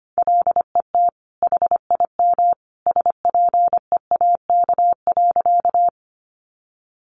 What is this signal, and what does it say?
Morse code, heard as a keyed tone, about 25 wpm, 700 Hz. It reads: LET HIM SPEAK.
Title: Sentences from Top 1100 Words 25wpm